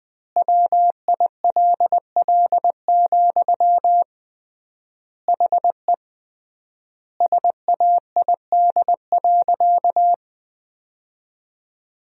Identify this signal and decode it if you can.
Morse code, heard as a keyed tone, about 20 wpm, 700 Hz. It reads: WILL, HE SAID.
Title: 3x Word-Spacing - Sentences from Top 200 Words - 20wpm